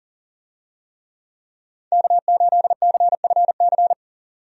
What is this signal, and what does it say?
Morse code, heard as a keyed tone, about 37 wpm, 700 Hz. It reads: K8CFC
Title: USA Call Signs - 5 Characters - 40wpm